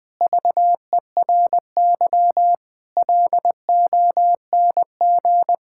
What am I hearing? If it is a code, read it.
Morse code, heard as a keyed tone, about 20 wpm, 700 Hz. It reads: VERY LONG
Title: Sentences from Top 1000 Words - Review Part 5 - 20wpm